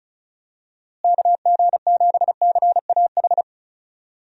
Morse code, heard as a keyed tone, about 36 wpm, 700 Hz. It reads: KG7CAH
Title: USA Call Signs - 6 Characters - Repeated 2x - 35wpm